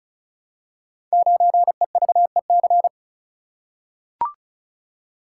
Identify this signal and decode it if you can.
Morse code, heard as a keyed tone, about 36 wpm, 690 Hz. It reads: 9EVEC
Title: Code Groups - 5 Letter:Numbers - 35wpm